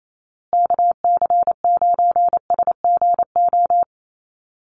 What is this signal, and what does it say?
Morse code, heard as a keyed tone, about 28 wpm, 700 Hz. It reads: KC9SGO
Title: USA Call Signs Repeated 3x 28wpm